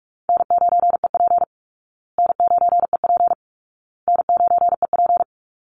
Morse code, heard as a keyed tone, about 44 wpm, 700 Hz. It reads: N9EP N9EP N9EP
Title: Easy Call Signs Repeated 3x - 45wpm